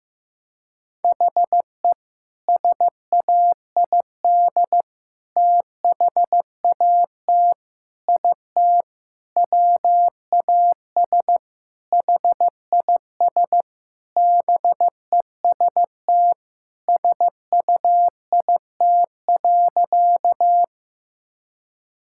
Morse code, heard as a keyed tone, about 15 wpm, 700 Hz. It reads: HE SAID THAT IT WAS HIS BEST SUIT.
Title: Sentences from Top 1600 Words 15wpm